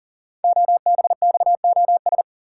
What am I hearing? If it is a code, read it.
Morse code, heard as a keyed tone, about 41 wpm, 700 Hz. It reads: OBXOS
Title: Code Groups - 5 Letters - 40wpm